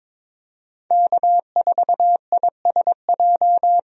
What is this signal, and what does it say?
Morse code, heard as a keyed tone, about 21 wpm, 700 Hz. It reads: K4ISJ
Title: USA Call Signs - 5 Characters - Repeated 2x - 22wpm